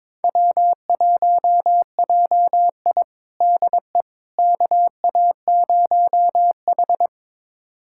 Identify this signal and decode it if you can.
Morse code, heard as a keyed tone, about 21 wpm, 700 Hz. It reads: W1JI DE KA0H
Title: USA Callsign DE Callsign - 4 Characters - 22wpm